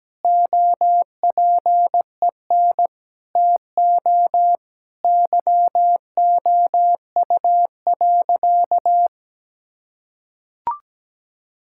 Morse code, heard as a keyed tone, about 17 wpm, 700 Hz. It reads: OPEN TO YOU.
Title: Sentences from Top 1600 Words 17wpm